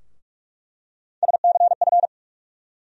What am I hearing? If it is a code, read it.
Morse code, heard as a keyed tone, about 46 wpm, 700 Hz. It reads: SCF
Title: USA Call Sign Postfixes - 3 Letters 45wpm